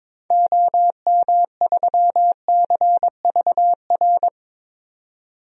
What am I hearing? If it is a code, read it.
Morse code, heard as a keyed tone, about 21 wpm, 700 Hz. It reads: OM3CVR
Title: DXCC-100 Call Signs - 6 Characters - 22wpm